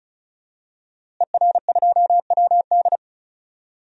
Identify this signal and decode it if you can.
Morse code, heard as a keyed tone, about 36 wpm, 700 Hz. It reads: ER2WD
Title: DXCC-100 Call Signs - 5 Characters - Repeated 3x - 35wpm